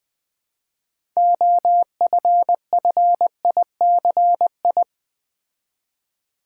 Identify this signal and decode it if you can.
Morse code, heard as a keyed tone, about 20 wpm, 700 Hz. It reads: OFFICI
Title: Build A Word - 2 Letters At A Time - 8-Letter Words - 20wpm